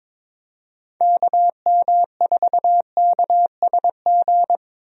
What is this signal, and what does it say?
Morse code, heard as a keyed tone, about 22 wpm, 700 Hz. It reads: KM4KSG
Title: USA Call Signs - 6 Characters - 22wpm